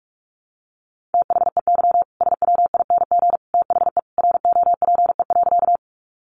Morse code, heard as a keyed tone, about 44 wpm, 700 Hz. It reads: THEY SWING THE ROPE.
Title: Easy Phrases 45wpm